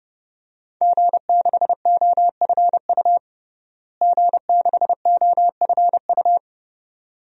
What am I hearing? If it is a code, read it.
Morse code, heard as a keyed tone, about 32 wpm, 700 Hz. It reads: G6OFU G6OFU
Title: DXCC-100 Call Signs - 5 Characters - Repeated 2x - 30wpm